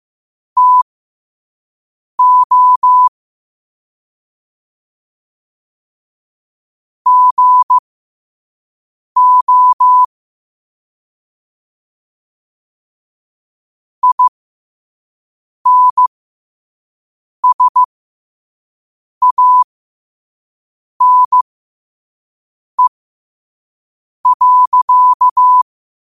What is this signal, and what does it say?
Morse code, heard as a keyed tone, about 15 wpm, spaced as for 5 wpm, 1000 Hz. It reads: TO GO INSANE.
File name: qotdfiles / 5wpm